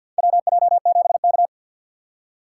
Morse code, heard as a keyed tone, about 52 wpm, 700 Hz. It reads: WJ7K
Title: USA Call Signs - 4 Characters - 3x Character Spacing - 50wpm